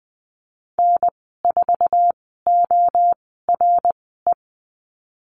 Morse code, heard as a keyed tone, about 20 wpm, spaced as for 15 wpm, 700 Hz. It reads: N4ORE
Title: 076 Lesson 13g - Letter L - Call Signs 20wpm